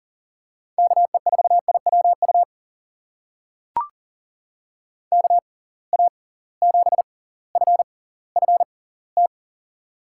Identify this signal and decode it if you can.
Morse code, heard as a keyed tone, about 40 wpm, spaced as for 20 wpm, 700 Hz. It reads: KE4IWU KA7FFT
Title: USA Call Signs - 6x Character Spacing - 40wpm